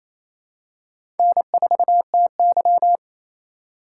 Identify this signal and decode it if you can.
Morse code, heard as a keyed tone, about 28 wpm, 700 Hz. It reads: N4TY
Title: USA Call Signs - 3x Character Spacing - 28wpm